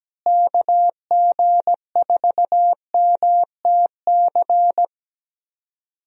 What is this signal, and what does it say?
Morse code, heard as a keyed tone, about 17 wpm, 700 Hz. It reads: KG4MTC
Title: USA Call Signs - 6 Characters - Repeated 3x - 17wpm